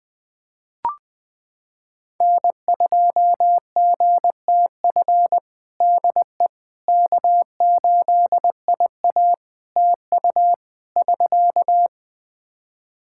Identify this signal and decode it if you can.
Morse code, heard as a keyed tone, about 20 wpm, 700 Hz. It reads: N2GTF DE K8IA TU <SK>
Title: Practice QSO - Exchange 4 - Ending - 20wpm